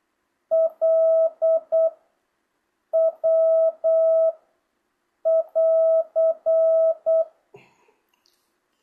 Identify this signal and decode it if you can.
Morse code, heard as a keyed tone, about 8 wpm, 660 Hz. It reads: L W <AR>
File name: jh8cbh / 2025cw